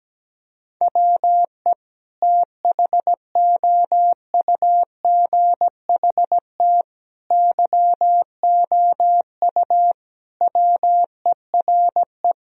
Morse code, heard as a keyed tone, about 17 wpm, 700 Hz. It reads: WE THOUGHT YOU WERE
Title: Sentences from Top 1000 Words - Review Part 4 - 17wpm